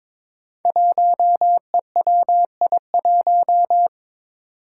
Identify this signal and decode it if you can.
Morse code, heard as a keyed tone, about 22 wpm, 700 Hz. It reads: 1EWI1
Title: Five Challenging Letter-Number Combinations 22wpm